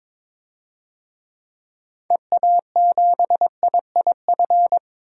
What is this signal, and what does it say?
Morse code, heard as a keyed tone, about 22 wpm, 700 Hz. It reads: EA7IIF
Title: DXCC-100 Call Signs - 6 Characters - 22wpm